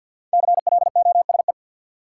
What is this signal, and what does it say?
Morse code, heard as a keyed tone, about 47 wpm, 700 Hz. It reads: CLOSE
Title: Five Letter Words - 50wpm